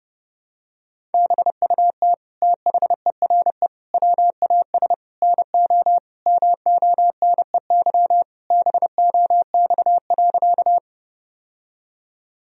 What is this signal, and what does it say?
Morse code, heard as a keyed tone, about 30 wpm, 700 Hz. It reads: BUT THERE WAS NO MONEY BOX.
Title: Sentences from Top 1000 Words - Review Part 2 - 30wpm